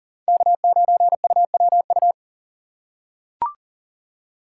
Morse code, heard as a keyed tone, about 38 wpm, 700 Hz. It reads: K9UWU
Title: USA Call Signs - 5 Characters - 2x Character Spacing - 40wpm